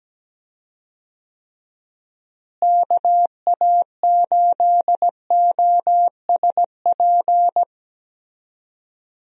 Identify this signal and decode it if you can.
Morse code, heard as a keyed tone, about 17 wpm, 700 Hz. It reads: KA8OSP
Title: USA Call Signs - 6 Characters - Repeated 2x - 17wpm